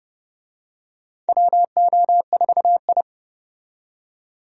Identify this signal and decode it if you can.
Morse code, heard as a keyed tone, about 32 wpm, 700 Hz. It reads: WO4I
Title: USA Call Signs - 4 Characters - 30wpm